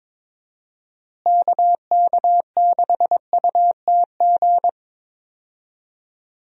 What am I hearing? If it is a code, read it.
Morse code, heard as a keyed tone, about 22 wpm, 700 Hz. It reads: KK6UTG
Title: USA Call Signs - 6 Characters - Repeated 3x - 22wpm